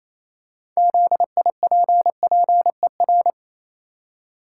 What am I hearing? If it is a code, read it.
Morse code, heard as a keyed tone, about 28 wpm, 700 Hz. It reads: ZIPPER
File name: Easy Words - Speed-Racing with Hints - 22wpm